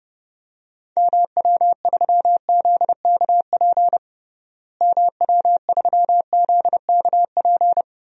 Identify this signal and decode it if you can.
Morse code, heard as a keyed tone, about 30 wpm, 700 Hz. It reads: MW3ZKP MW3ZKP
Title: DXCC-100 Call Signs - 6 Characters - Repeated 2x - 30wpm